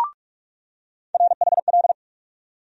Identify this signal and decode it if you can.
Morse code, heard as a keyed tone, about 46 wpm, 700 Hz. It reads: RHL
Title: USA Call Sign Postfixes - 3 Letters 45wpm